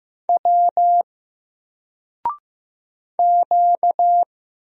Morse code, heard as a keyed tone, about 15 wpm, 700 Hz. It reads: W Q
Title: Single Letter-Number Alt 15wpm